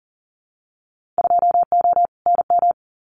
Morse code, heard as a keyed tone, about 38 wpm, 700 Hz. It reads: 2O NM
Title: Winter Field Day Exchanges 40wpm